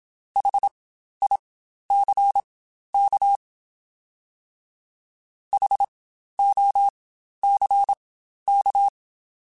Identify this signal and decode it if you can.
Morse code, heard as a keyed tone, about 27 wpm, spaced as for 12 wpm, 800 Hz. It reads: HICK HOCK
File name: Spaced Alphas